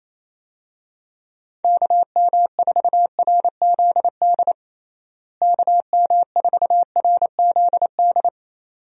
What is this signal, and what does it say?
Morse code, heard as a keyed tone, about 28 wpm, 700 Hz. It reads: KM4RZD KM4RZD
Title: USA Call Signs - 6 Characters - Repeated 2x - 28wpm